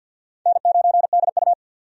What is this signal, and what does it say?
Morse code, heard as a keyed tone, about 51 wpm, 700 Hz. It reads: N9DU
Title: USA Call Signs - 4 Characters - 50wpm